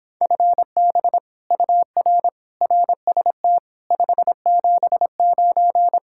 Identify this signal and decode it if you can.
Morse code, heard as a keyed tone, about 26 wpm, 700 Hz. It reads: FB UR RST 579